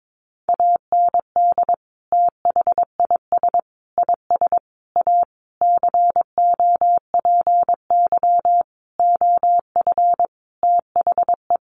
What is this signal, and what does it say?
Morse code, heard as a keyed tone, about 22 wpm, 700 Hz. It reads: AND THIS IS A COPY OF THE
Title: Sentences from Top 1400 Words 22wpm